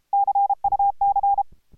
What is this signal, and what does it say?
Morse code, heard as a keyed tone, about 26 wpm, 810 Hz. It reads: CUC